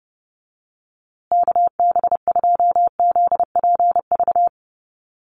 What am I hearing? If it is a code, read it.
Morse code, heard as a keyed tone, about 30 wpm, 700 Hz. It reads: KB2ZPV
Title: USA Call Signs Repeated 2x 30wpm